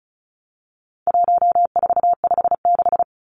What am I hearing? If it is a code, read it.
Morse code, heard as a keyed tone, about 34 wpm, 700 Hz. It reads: 1456
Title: Numbers - 4-Digit 35wpm